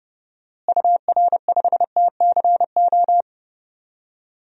Morse code, heard as a keyed tone, about 30 wpm, 700 Hz. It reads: UR5TCO
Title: DXCC-100 Call Signs - 6 Characters - Repeated 2x - 30wpm